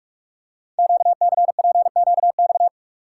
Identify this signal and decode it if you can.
Morse code, heard as a keyed tone, about 44 wpm, 690 Hz. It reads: QCPQX
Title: Code Groups - 5 Letters - 45wpm